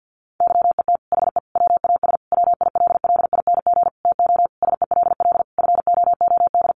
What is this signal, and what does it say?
Morse code, heard as a keyed tone, about 50 wpm, 700 Hz. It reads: YET HE WAS WILLING TO SELL FOOD
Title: Sentences from Top 1100 Words 50wpm